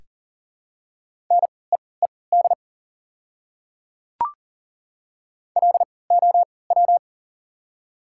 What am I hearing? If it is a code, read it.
Morse code, heard as a keyed tone, about 40 wpm, spaced as for 23 wpm, 690 Hz. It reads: NEED LOW
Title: 3x Character Spacing - Top 200 Words - 40wpm